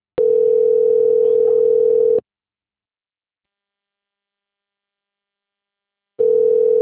Ringback tone (North America), 440 + 480 Hz.